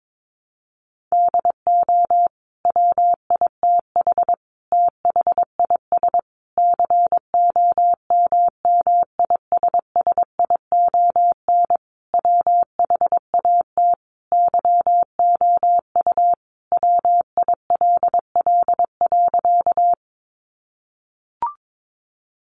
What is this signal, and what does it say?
Morse code, heard as a keyed tone, about 22 wpm, 700 Hz. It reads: DO WITH THIS COMMISSION WHAT YOU WILL.
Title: Sentences from Top 1400 Words 22wpm